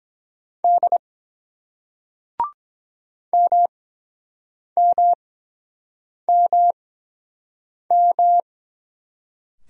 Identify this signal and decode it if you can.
Morse code, heard as a keyed tone, about 22 wpm, spaced as for 7 wpm, 700 Hz. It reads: D MMMM